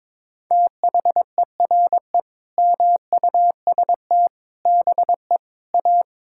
Morse code, heard as a keyed tone, about 22 wpm, 700 Hz. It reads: THERE MUST BE A
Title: Sentences from Top 1000 Words - Review Part 6 - 22wpm